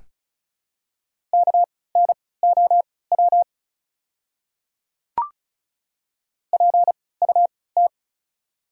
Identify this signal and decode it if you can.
Morse code, heard as a keyed tone, about 34 wpm, spaced as for 20 wpm, 700 Hz. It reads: KNOW PUT
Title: 3x Character Spacing - Top 300 Words - 35wpm